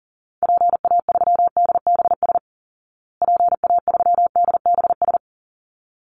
Morse code, heard as a keyed tone, about 40 wpm, 700 Hz. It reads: PA3DBS PA3DBS
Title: Call Signs Repeated 3x - 40wpm